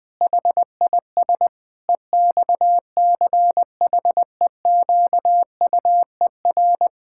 Morse code, heard as a keyed tone, about 20 wpm, 700 Hz. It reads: HIS EXCHEQUER
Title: Pangram Challenge v2 - 20wpm